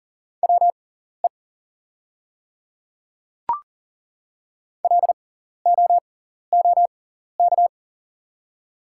Morse code, heard as a keyed tone, about 40 wpm, spaced as for 14 wpm, 700 Hz. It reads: WE LOOK
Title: 6x Character Spacing - Top 500 Words - 40wpm